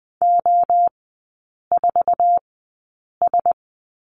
Morse code, heard as a keyed tone, about 20 wpm, spaced as for 8 wpm, 700 Hz. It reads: O4S
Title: 047 Lesson 9e - Number 4 - Three Characters 20wpm